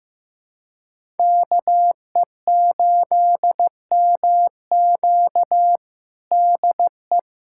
Morse code, heard as a keyed tone, about 15 wpm, 700 Hz. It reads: KE8MQ DE
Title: USA Callsign DE Callsign 15wpm